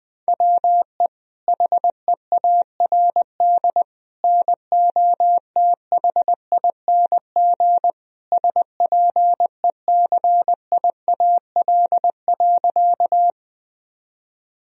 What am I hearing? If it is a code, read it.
Morse code, heard as a keyed tone, about 21 wpm, 700 Hz. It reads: WE HEARD NOTHING SPECIAL.
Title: Sentences from Top 1000 Words - Review Part 3 - 20wpm